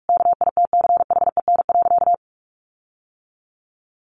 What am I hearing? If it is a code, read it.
Morse code, heard as a keyed tone, about 44 wpm, 700 Hz. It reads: KITCHEN.
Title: Sentences from Top 1500 Words 45wpm